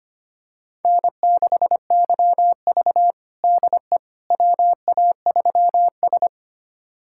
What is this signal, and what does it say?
Morse code, heard as a keyed tone, about 25 wpm, 700 Hz. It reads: N6YV DE WA3S
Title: USA Callsign DE Callsign - 4 Characters - 25wpm